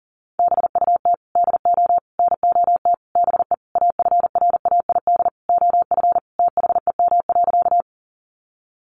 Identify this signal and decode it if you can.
Morse code, heard as a keyed tone, about 40 wpm, 700 Hz. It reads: BUT DO NOT BE AFRAID OF THEM.
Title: Sentences from Top 1500 Words 40wpm